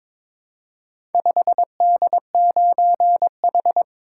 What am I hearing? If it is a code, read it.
Morse code, heard as a keyed tone, about 21 wpm, 700 Hz. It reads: 5D9H
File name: practice / 22wpm / Four Letter-Number Combinations - 22wpm